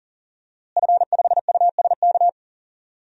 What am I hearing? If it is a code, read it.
Morse code, heard as a keyed tone, about 40 wpm, 700 Hz. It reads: F5USK
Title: DXCC-100 Call Signs - 5 Characters - Repeated 2x - 40wpm